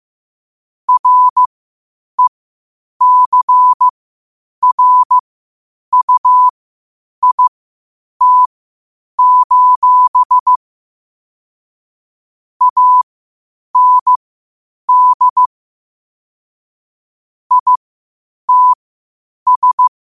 Morse code, heard as a keyed tone, about 15 wpm, spaced as for 8 wpm, 1000 Hz. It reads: RECRUIT: AND ITS
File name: qotdfiles / 8wpm